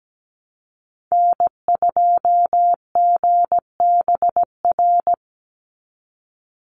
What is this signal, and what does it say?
Morse code, heard as a keyed tone, about 17 wpm, 700 Hz. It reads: N2GBR
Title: NA SOTA Chaser Callsigns 17wpm